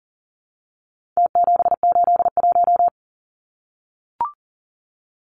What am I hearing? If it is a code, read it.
Morse code, heard as a keyed tone, about 37 wpm, 700 Hz. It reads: T781
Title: Numbers - Contest Style - 4-Digit 40wpm